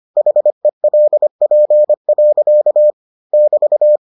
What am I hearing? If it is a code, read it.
Morse code, heard as a keyed tone, about 25 wpm, 600 Hz. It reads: HELP. <BT>